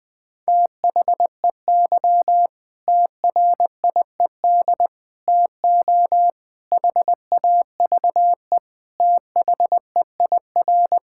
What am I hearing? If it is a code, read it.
Morse code, heard as a keyed tone, about 20 wpm, 700 Hz. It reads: THEY TRIED TO HAVE THEIR
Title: Sentences with 8-Word Phrases - 20wpm